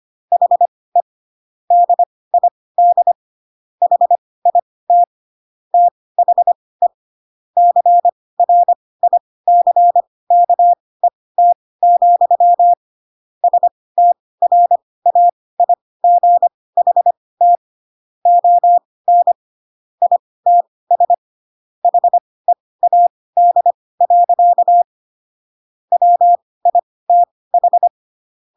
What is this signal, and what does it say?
Morse code, heard as a keyed tone, about 25 wpm, spaced as for 18 wpm, 700 Hz. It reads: HE DID HIT THE CRICKET, STRAIGHT ON ITS HEAD. WITH
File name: Pinocchio / Pinocchio ch4 12 25 ik1hge